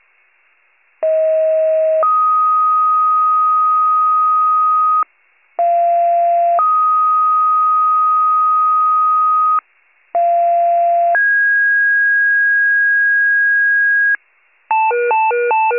Two-tone page, heard as a one-second tone, then a three-second tone, about 640 then 1250 Hz, about 690 then 1250 Hz, about 690 then 1650 Hz, with noise. It is heard from 1.0 to 14.2 s.